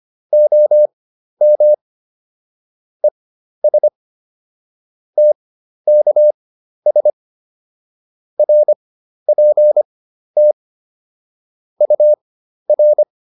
Morse code, heard as a keyed tone, about 25 wpm, spaced as for 12 wpm, 600 Hz. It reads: OM ES TKS RPT UR
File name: QSO 203 12